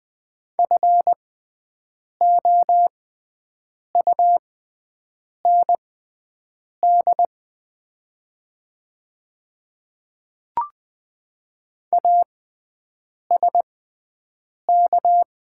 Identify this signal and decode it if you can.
Morse code, heard as a keyed tone, about 20 wpm, spaced as for 7 wpm, 700 Hz. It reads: FOUND ASK
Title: 6x Character Spacing - Top 300 Words - 20wpm